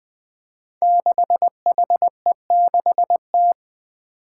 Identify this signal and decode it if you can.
Morse code, heard as a keyed tone, about 20 wpm, 700 Hz. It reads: 6HE6T
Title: Five Challenging Letter-Number Combinations 20wpm